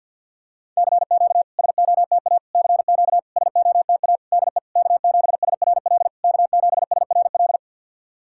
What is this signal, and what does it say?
Morse code, heard as a keyed tone, about 48 wpm, 690 Hz. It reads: CQ SOTA CQ SOTA DE K7SRL K7SRL
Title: SOTA QSO Practice 50wpm